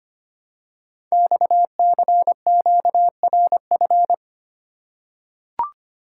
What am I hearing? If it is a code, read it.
Morse code, heard as a keyed tone, about 25 wpm, 700 Hz. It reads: XCQRF
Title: Code Groups - 5 Letters - 25wpm